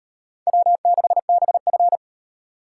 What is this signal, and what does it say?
Morse code, heard as a keyed tone, about 36 wpm, 700 Hz. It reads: W6BF